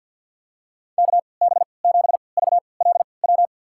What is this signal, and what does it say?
Morse code, heard as a keyed tone, about 51 wpm, spaced as for 28 wpm, 690 Hz. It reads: KB7VLW
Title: USA Call Signs - 6 Characters - 3x Character Spacing - 50wpm